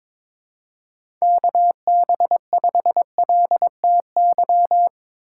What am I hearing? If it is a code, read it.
Morse code, heard as a keyed tone, about 23 wpm, 700 Hz. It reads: KB5LTY
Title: USA Call Signs - 6 Characters - 3x Character Spacing - 22wpm